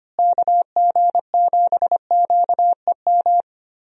Morse code, heard as a keyed tone, about 25 wpm, 700 Hz. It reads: KG7QEM
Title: USA Call Signs - 6 Characters - Repeated 3x - 25wpm